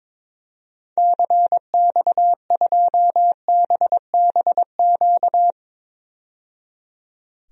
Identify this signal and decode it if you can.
Morse code, heard as a keyed tone, about 21 wpm, 700 Hz. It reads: CX2BBQ
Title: DXCC-100 Call Signs - 6 Characters - 22wpm